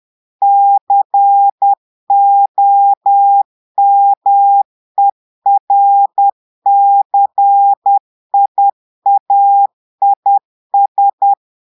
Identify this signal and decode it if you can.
Morse code, heard as a keyed tone, about 10 wpm, 800 Hz. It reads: COMERCIAIS
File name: portugais texte en clair 2 vitesse 10 mots minutes